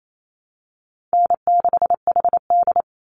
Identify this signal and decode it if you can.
Morse code, heard as a keyed tone, about 28 wpm, 700 Hz. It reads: N6HD
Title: Easy Call Signs Repeated 2x - 28wpm